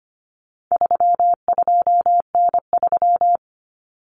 Morse code, heard as a keyed tone, about 25 wpm, 700 Hz. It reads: 32N3
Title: Common Cut Numbers - 4-Digit 25wpm